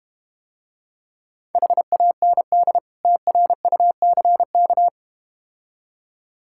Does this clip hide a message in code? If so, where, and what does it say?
Yes: Morse code from 1.6 to 4.9 s, heard as a keyed tone, about 32 wpm, 700 Hz. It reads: HAND TRUCK